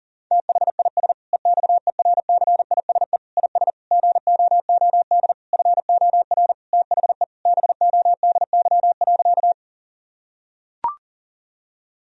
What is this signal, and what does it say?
Morse code, heard as a keyed tone, about 40 wpm, 700 Hz. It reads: THIS EXERCISE IS GOOD FOR THE BODY.
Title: Sentences from Top 1000 Words - Review Part 4 - 40wpm